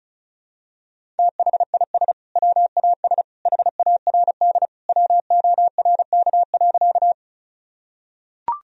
Morse code, heard as a keyed tone, about 36 wpm, 700 Hz. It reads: THIS WAS HARD WORK.
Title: Sentences from Top 1000 Words - Review Part 1 - 35wpm